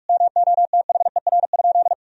Morse code, heard as a keyed tone, about 46 wpm, 700 Hz. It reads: MOTHER?
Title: Sentences from Top 1000 Words - Review Part 2 - 45wpm